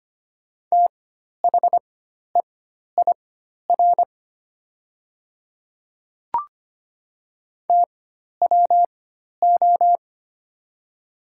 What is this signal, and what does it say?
Morse code, heard as a keyed tone, about 25 wpm, spaced as for 12 wpm, 700 Hz. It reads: THEIR TWO
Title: 4x Character Spacing - Top 100 Words - 25wpm